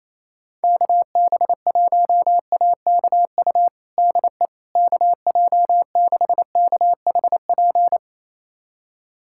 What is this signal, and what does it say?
Morse code, heard as a keyed tone, about 27 wpm, 700 Hz. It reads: KB1AKU DE KJ6KHP
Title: USA Callsign DE Callsign - 6 Characters - 28wpm